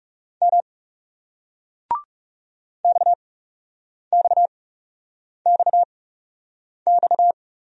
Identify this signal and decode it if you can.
Morse code, heard as a keyed tone, about 36 wpm, spaced as for 8 wpm, 700 Hz. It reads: M XXXX
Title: Speed-Racing Single Letters 30wpm